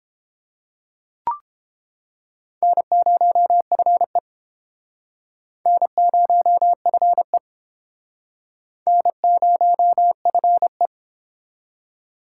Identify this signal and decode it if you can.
Morse code, heard as a keyed tone, about 29 wpm, 700 Hz. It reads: N0FE N0FE N0FE